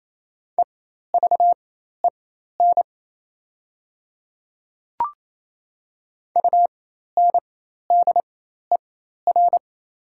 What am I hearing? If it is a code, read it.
Morse code, heard as a keyed tone, about 27 wpm, spaced as for 13 wpm, 700 Hz. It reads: EVEN UNDER
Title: 4x Character Spacing - Top 200 Words - 28wpm